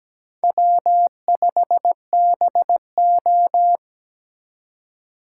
Morse code, heard as a keyed tone, about 20 wpm, 700 Hz. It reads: W5BO
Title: Speed Racing - USA 1x2 Call Signs 17wpm